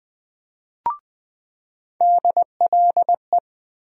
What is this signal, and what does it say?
Morse code, heard as a keyed tone, about 20 wpm, 700 Hz. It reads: DLE
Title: USA Call Sign Postfixes - 3 Letters 20wpm